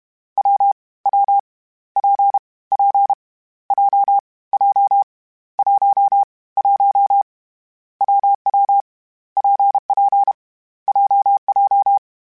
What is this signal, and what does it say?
Morse code, heard as a keyed tone, about 32 wpm, 800 Hz. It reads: W W P P J J 1 1 WW PP JJ
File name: Difficult Combos / Difficult Combos-mp3s / WPJ WPJ1 @30